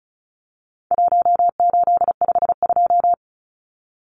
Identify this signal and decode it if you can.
Morse code, heard as a keyed tone, about 34 wpm, 700 Hz. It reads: 1852
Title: Numbers - 4-Digit 35wpm